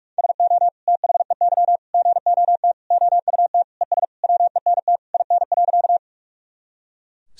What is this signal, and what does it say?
Morse code, heard as a keyed tone, about 46 wpm, 700 Hz. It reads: SO THEY GOT OUT ES WENT IN.
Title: Sentences from Top 300 Words and Abbreviations 45wpm